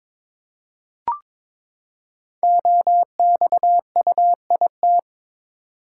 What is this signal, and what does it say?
Morse code, heard as a keyed tone, about 22 wpm, 700 Hz. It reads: OXUIT
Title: Code Groups - 5 Letters - 22wpm